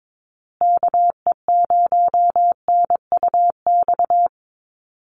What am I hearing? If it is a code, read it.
Morse code, heard as a keyed tone, about 22 wpm, 700 Hz. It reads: KE0NUX
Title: USA Call Signs Repeated 3x 22wpm